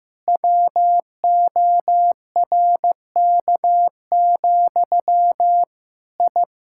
Morse code, heard as a keyed tone, about 15 wpm, 700 Hz. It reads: WORK, I
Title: Sentences from Top 1000 Words - Review Part 6 - 15wpm